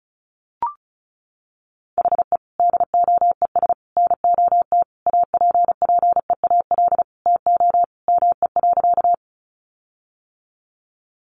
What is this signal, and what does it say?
Morse code, heard as a keyed tone, about 34 wpm, 700 Hz. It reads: HE DOES NOT APPEAL TO ME.
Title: Sentences from Top 1300 Words 35wpm